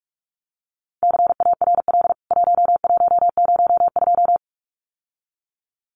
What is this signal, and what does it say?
Morse code, heard as a keyed tone, about 44 wpm, 700 Hz. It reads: CARL 1102
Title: CWT Exchange 45wpm